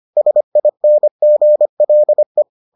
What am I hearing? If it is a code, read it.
Morse code, heard as a keyed tone, about 25 wpm, 600 Hz. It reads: SINGLE